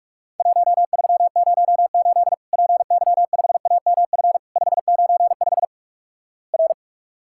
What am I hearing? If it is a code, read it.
Morse code, heard as a keyed tone, about 45 wpm, 700 Hz. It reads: 1308 PY5AMF 595 R
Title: Worked All Europe DX Contest - QTC Exchange - 45wpm